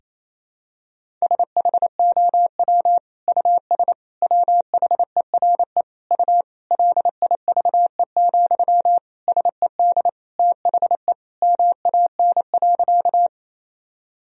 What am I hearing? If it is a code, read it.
Morse code, heard as a keyed tone, about 28 wpm, 700 Hz. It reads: SHOW US WHERE U LIVE, SED THE MAN.
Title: Sentences from Top 200 Words and Abbreviations 28wpm